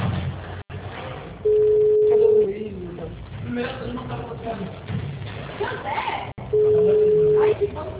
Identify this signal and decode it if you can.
Ringback tone (Europe, 425 Hz), with speech.
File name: out- -2006- - - .72097